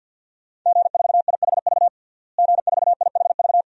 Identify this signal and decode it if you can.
Morse code, heard as a keyed tone, about 48 wpm, 700 Hz. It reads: G4IHV G4IHV
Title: DXCC-100 Call Signs - 5 Characters - Repeated 2x - 50wpm